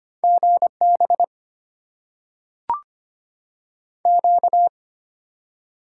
Morse code, heard as a keyed tone, about 25 wpm, 700 Hz. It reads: GB Q
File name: Top 90 QSO Elements - 8x Character Spacing - 25wpm